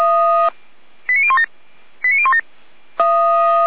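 Five-tone selective call (ZVEI), heard as quick runs of five tones, tones of about 70 ms. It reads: 89017 89017
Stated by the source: ZVEI five-tone selective call received by SDR.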